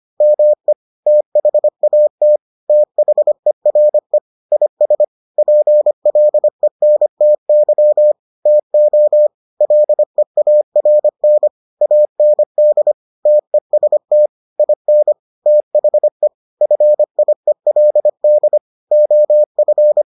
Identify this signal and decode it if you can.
Morse code, heard as a keyed tone, about 25 wpm, 600 Hz. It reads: ME THAT THERE IS PLENTY TO LEARN AND TEST IN THE FIELD OF